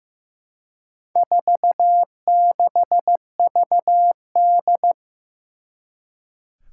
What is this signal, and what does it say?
Morse code, heard as a keyed tone, about 15 wpm, 700 Hz. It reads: 46VD
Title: Four Challenging Letter-Number Combinations 15wpm